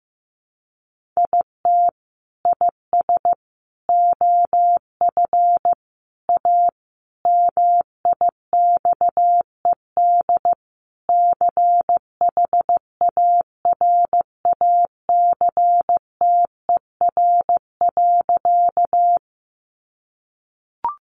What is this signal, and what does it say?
Morse code, heard as a keyed tone, about 15 wpm, 700 Hz. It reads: IT IS OF A MIXED CHARACTER.
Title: Sentences from Top 1500 Words 15wpm